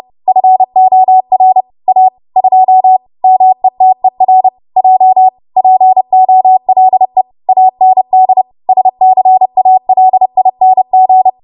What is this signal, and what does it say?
Morse code, heard as a keyed tone, about 30 wpm, 750 Hz. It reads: FOR A 2 METER J POLE AND SCALING
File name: Morse / 30WPM